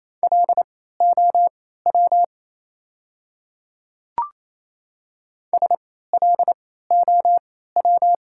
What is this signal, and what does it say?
Morse code, heard as a keyed tone, about 28 wpm, spaced as for 16 wpm, 690 Hz. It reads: LOW SLOW
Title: 3x Character Spacing - Top 500 Words - 28wpm